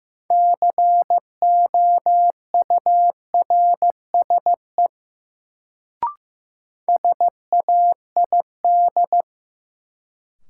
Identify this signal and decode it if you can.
Morse code, heard as a keyed tone, about 15 wpm, 700 Hz. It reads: COURSE SAID
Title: Sets of 2 Words - New Word Drill - 15wpm